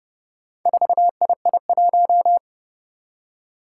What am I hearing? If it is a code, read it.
Morse code, heard as a keyed tone, about 30 wpm, 700 Hz. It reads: 4II1
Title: Four Challenging Letter-Number Combinations 30wpm